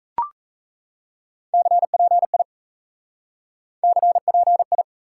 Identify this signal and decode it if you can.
Morse code, heard as a keyed tone, about 37 wpm, 700 Hz. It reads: CPI CPI